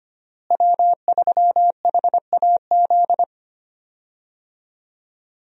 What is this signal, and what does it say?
Morse code, heard as a keyed tone, about 25 wpm, 700 Hz. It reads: W3HAZ
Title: USA Call Signs - 5 Characters - Repeated 3x - 25wpm